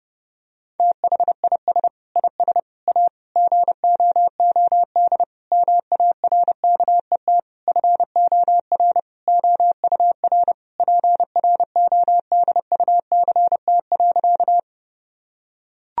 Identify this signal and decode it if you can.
Morse code, heard as a keyed tone, about 30 wpm, 700 Hz. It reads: THIS IS A GOOD MARKET FOR OUR PRODUCT.
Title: Sentences from Top 1000 Words - Review Part 3 - 30wpm